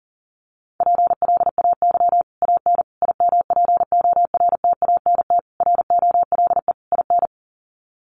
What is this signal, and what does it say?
Morse code, heard as a keyed tone, about 37 wpm, 700 Hz. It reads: PLAY AN IMPORTANT ROLE IN
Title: Sets of 5 Words Encore 40wpm